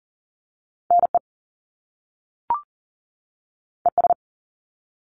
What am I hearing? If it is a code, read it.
Morse code, heard as a keyed tone, about 40 wpm, 700 Hz. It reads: NE ES
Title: Top Two Letter Combinations in Words 40wpm